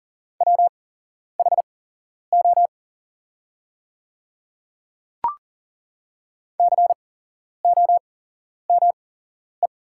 Morse code, heard as a keyed tone, about 40 wpm, spaced as for 11 wpm, 690 Hz. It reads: WHO COME
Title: 8x Character Spacing - Top 100 Words - 40wpm